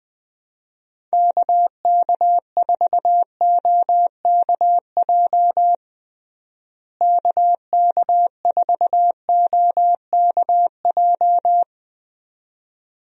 Morse code, heard as a keyed tone, about 20 wpm, 700 Hz. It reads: KK4OKJ KK4OKJ
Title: USA Call Signs - 6 Characters - Repeated 2x - 20wpm